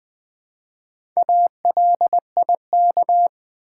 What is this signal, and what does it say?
Morse code, heard as a keyed tone, about 20 wpm, 700 Hz. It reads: ALIK
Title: Build A Word - 1 Letter At A Time - 3 to 5-Letter Words - 20wpm